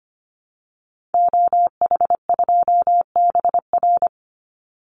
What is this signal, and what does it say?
Morse code, heard as a keyed tone, about 25 wpm, 700 Hz. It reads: OH2BR
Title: Call Signs - 25wpm